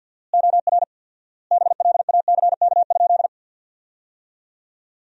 Morse code, heard as a keyed tone, about 49 wpm, 700 Hz. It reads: OR BLACK?
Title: 4x Word-Spacing - Sentences from Top 400 Words - 50wpm